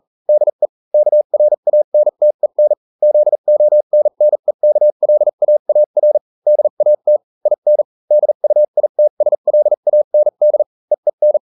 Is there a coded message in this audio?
Yes: Morse code from 0.3 to 11.4 s, heard as a keyed tone, about 40 wpm, spaced as for 35 wpm, 600 Hz. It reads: DE KRANTEN ZONNEKLAAR DAT IN DUITSLAND EEN